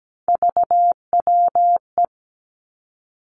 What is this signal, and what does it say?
Morse code, heard as a keyed tone, about 17 wpm, 700 Hz. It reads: VWE
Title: Three Challenging Letter-Number Combinations 17wpm